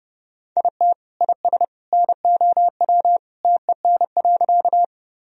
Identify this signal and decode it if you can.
Morse code, heard as a keyed tone, about 32 wpm, 700 Hz. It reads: IT IS NOW TEN.
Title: Sentences from Top 1000 Words - Review Part 2 - 30wpm